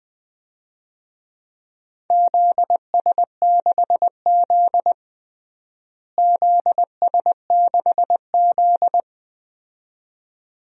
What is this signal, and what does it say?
Morse code, heard as a keyed tone, about 20 wpm, 700 Hz. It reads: ZS6Z ZS6Z
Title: DXCC-100 Call Signs - 4 Characters - Repeated 2x - 20wpm